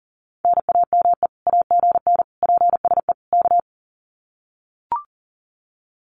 Morse code, heard as a keyed tone, about 40 wpm, 700 Hz. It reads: NAME AGN PSE K
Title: Common QSO Phrases Encore 40wpm